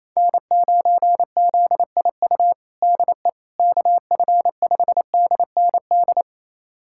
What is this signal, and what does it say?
Morse code, heard as a keyed tone, about 28 wpm, 700 Hz. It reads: N9ZIU DE KF5DND
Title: USA Callsign DE Callsign 28wpm